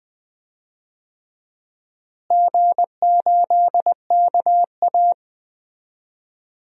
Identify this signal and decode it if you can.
Morse code, heard as a keyed tone, about 20 wpm, 700 Hz. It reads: G8KA
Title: DXCC-100 Call Signs - 4 Characters - Repeated 2x - 20wpm